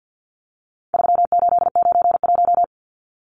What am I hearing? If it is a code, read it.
Morse code, heard as a keyed tone, about 50 wpm, 700 Hz. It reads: 3891
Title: Numbers - 4-Digit 50wpm